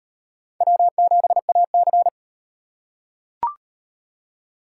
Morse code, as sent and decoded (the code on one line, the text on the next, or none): .-- --... .- -.-.
W7AC